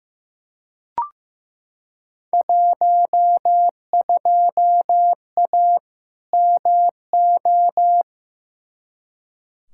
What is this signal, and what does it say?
Morse code, heard as a keyed tone, about 15 wpm, 700 Hz. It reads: 12A MO
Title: Field Day Exchanges Encore - 15wpm